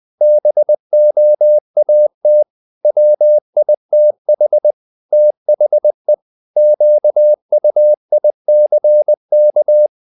Morse code, heard as a keyed tone, about 20 wpm, 600 Hz. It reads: BOAT WITH THE QUICK
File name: the old man and the sea / the old man and the sea part 1 20wpm